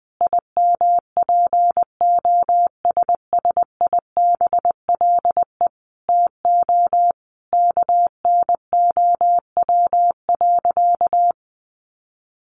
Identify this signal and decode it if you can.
Morse code, heard as a keyed tone, about 20 wpm, 700 Hz. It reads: IMPOSSIBLE TO KNOW.
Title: Sentences from Top 1100 Words 20wpm